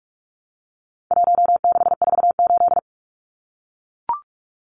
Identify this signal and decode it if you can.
Morse code, heard as a keyed tone, about 44 wpm, 700 Hz. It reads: 1648
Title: Numbers - 4-Digit 45wpm